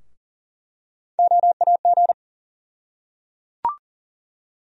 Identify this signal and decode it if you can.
Morse code, heard as a keyed tone, about 39 wpm, 700 Hz. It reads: OAG
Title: USA Call Sign Postfixes - 3 Letters 40wpm